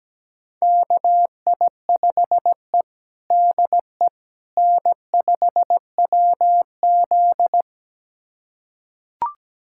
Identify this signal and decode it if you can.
Morse code, heard as a keyed tone, about 17 wpm, 700 Hz. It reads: KI5E DE N5WZ
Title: USA Callsign DE Callsign - 4 Characters - 17wpm